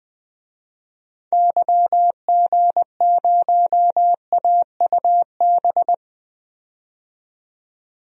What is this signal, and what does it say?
Morse code, heard as a keyed tone, about 21 wpm, 700 Hz. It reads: YG0AUB
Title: DXCC-100 Call Signs - 6 Characters - 20wpm